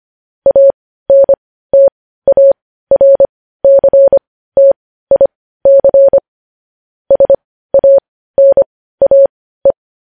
Morse code, heard as a keyed tone, about 25 wpm, spaced as for 15 wpm, 550 Hz. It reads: ANTARCTIC SANAE
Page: news headlines in morse code at 15 wpm